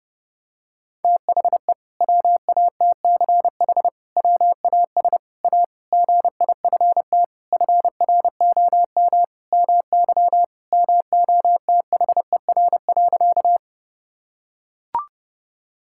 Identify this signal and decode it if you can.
Morse code, heard as a keyed tone, about 30 wpm, 700 Hz. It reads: THE WATCH WAS A GIFT FROM MY MOTHER.
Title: Sentences from Top 1000 Words - Review Part 6 - 30wpm